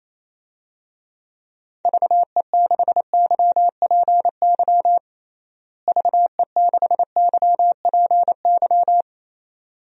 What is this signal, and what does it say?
Morse code, heard as a keyed tone, about 28 wpm, 700 Hz. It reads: VE6YPY VE6YPY
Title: DXCC-100 Call Signs - 6 Characters - Repeated 2x - 28wpm